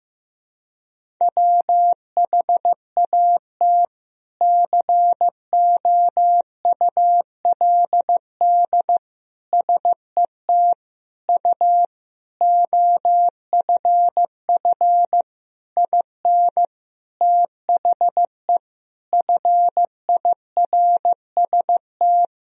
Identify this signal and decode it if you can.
Morse code, heard as a keyed tone, about 16 wpm, 700 Hz. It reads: WHAT COULD SET U OFF IN THE FIRST
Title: Sentences from Top 200 Words and Abbreviations 15wpm